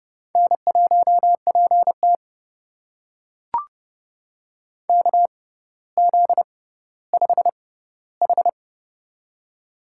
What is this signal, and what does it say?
Morse code, heard as a keyed tone, about 30 wpm, spaced as for 15 wpm, 700 Hz. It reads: N1PT KZ5H
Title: USA Call Signs - 4 Characters - 6x Character Spacing - 30wpm